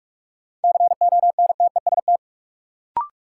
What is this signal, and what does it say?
Morse code, heard as a keyed tone, about 46 wpm, 700 Hz. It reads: CONTEST
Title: Sentences from Top 1600 Words - New Word Drill - 45wpm